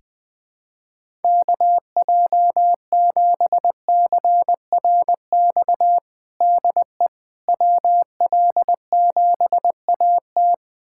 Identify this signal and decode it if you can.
Morse code, heard as a keyed tone, about 21 wpm, 700 Hz. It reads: KJ7CRX DE WL7AT
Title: USA Callsign DE Callsign 20wpm